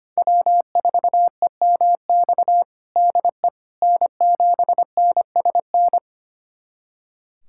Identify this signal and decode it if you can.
Morse code, heard as a keyed tone, about 25 wpm, 700 Hz. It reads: W4EMX DE N7NSN
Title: USA Callsign DE Callsign - 5 Characters - 25wpm